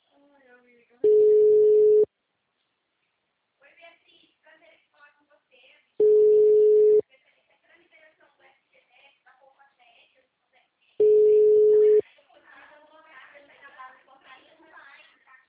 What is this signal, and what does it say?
Ringback tone (Europe, 425 Hz), with speech.